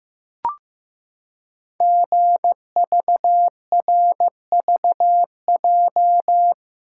Morse code, heard as a keyed tone, about 15 wpm, 700 Hz. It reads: GVRVJ
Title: Code Groups - 5 Letter:Numbers - 15wpm